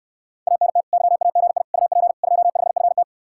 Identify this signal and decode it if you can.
Morse code, heard as a keyed tone, about 69 wpm, 700 Hz. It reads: ANT YAGI UP 25FT